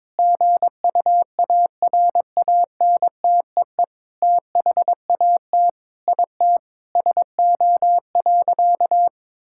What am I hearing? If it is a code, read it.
Morse code, heard as a keyed tone, about 21 wpm, 700 Hz. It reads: GUARANTEE THAT IT SO.
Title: Sentences from Top 1600 Words 22wpm